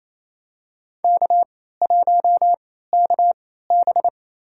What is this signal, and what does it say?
Morse code, heard as a keyed tone, about 28 wpm, spaced as for 16 wpm, 700 Hz. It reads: K1KB
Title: USA Call Signs - 4 Characters - 3x Character Spacing - 28wpm